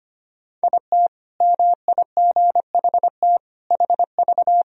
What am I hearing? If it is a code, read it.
Morse code, heard as a keyed tone, about 25 wpm, 700 Hz. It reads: IT MIGHT HV
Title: Sentences from Top 500 Words and Abbreviations 25wpm